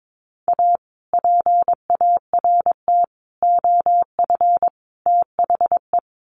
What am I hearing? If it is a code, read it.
Morse code, heard as a keyed tone, about 22 wpm, 700 Hz. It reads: A PART OF THE
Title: Sets of 4 Words 22wpm